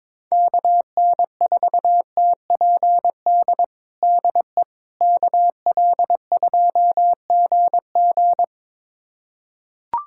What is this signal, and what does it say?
Morse code, heard as a keyed tone, about 22 wpm, 700 Hz. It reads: KN4TPD DE KL2GG
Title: USA Callsign DE Callsign 22wpm